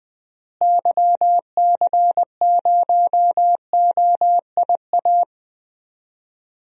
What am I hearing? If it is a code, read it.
Morse code, heard as a keyed tone, about 20 wpm, 700 Hz. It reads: YC0OIA
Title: DXCC-100 Call Signs - 6 Characters - Repeated 2x - 20wpm